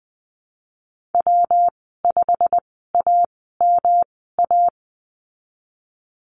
Morse code, heard as a keyed tone, about 20 wpm, spaced as for 15 wpm, 700 Hz. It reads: W5AMA
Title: 147 Lesson 24g - Letter y - Call Signs 20wpm